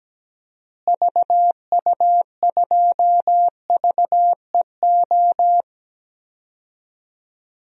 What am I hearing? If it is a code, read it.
Morse code, heard as a keyed tone, about 17 wpm, 700 Hz. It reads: VU2VEO
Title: DXCC-100 Call Signs - 6 Characters - Repeated 2x - 17wpm